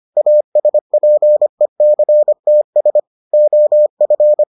Morse code, heard as a keyed tone, about 25 wpm, 600 Hz. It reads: ASPECTS OF